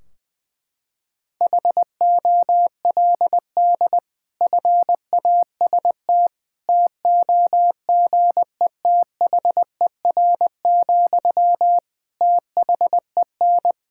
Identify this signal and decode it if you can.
Morse code, heard as a keyed tone, about 20 wpm, 700 Hz. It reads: HOLD FAST TOGETHER, THEN
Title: Sentences from Top 1600 Words 20wpm